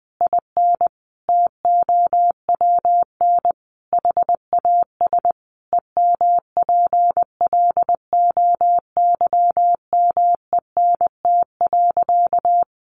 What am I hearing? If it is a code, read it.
Morse code, heard as a keyed tone, about 20 wpm, 700 Hz. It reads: IN TOWN HAS EMPLOYMENT.
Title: Sentences from Top 1500 Words 20wpm